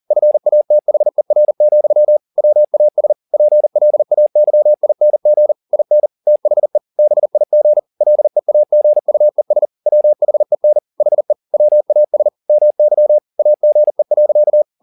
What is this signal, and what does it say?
Morse code, heard as a keyed tone, about 40 wpm, 600 Hz. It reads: FATHER, WAS PLAYING IN THE BIG LEAGUES WHEN HE WAS MY AGE.